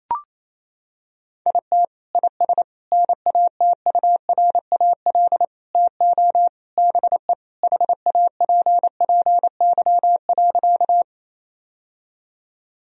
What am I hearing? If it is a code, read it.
Morse code, heard as a keyed tone, about 28 wpm, 700 Hz. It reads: IT IS NATURAL TO BE HAPPY.
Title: Sentences from Top 1000 Words - Review Part 4 - 28wpm